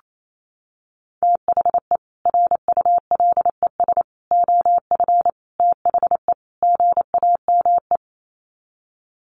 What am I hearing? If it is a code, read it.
Morse code, heard as a keyed tone, about 28 wpm, 700 Hz. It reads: THE RULES OF THE GAME
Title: Sets of 5 Words Encore2 28wpm